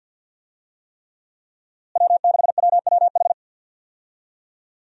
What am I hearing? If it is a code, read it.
Morse code, heard as a keyed tone, about 50 wpm, 700 Hz. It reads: W6WWH
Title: USA Call Signs - 5 Characters - Repeated 3x - 50wpm